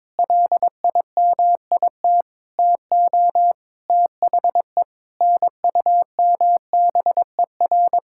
Morse code, heard as a keyed tone, about 22 wpm, 700 Hz. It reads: LIMIT TO THE NUMBER
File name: practice / 22wpm / Sets of 8 Words - 22wpm